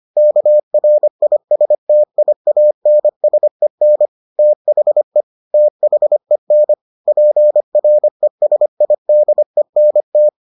Morse code, heard as a keyed tone, about 25 wpm, 600 Hz. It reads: KRISTIANSEN THE THEN PRESIDENT